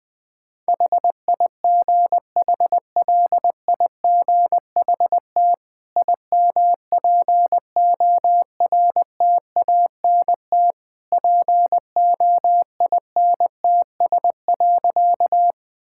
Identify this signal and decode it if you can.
Morse code, heard as a keyed tone, about 20 wpm, 700 Hz. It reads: HIGHLIGHT IMPORTANT POINTS.
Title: Easy Sentences with Hints - 20wpm